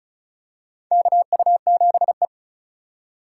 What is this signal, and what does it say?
Morse code, heard as a keyed tone, about 36 wpm, 690 Hz. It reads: KU7E
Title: USA Call Signs - 4 Characters - 2x Character Spacing - 35wpm